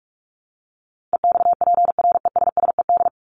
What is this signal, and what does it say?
Morse code, heard as a keyed tone, about 44 wpm, 700 Hz. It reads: EXPRESSED
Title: ED Suffix - Top 100 Words with Base Words 45wpm